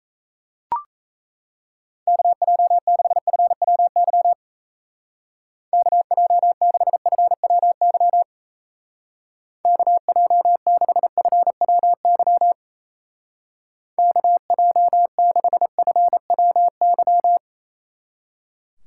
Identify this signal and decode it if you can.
Morse code, heard as a keyed tone, about 34 wpm, 700 Hz. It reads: KJ6FWY KJ6FWY KJ6FWY KJ6FWY
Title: Speed Racing - USA Call Signs 28wpm